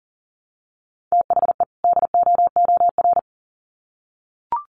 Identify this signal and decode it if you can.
Morse code, heard as a keyed tone, about 40 wpm, 700 Hz. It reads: THE DOOR
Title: Sets of 2 Words 40wpm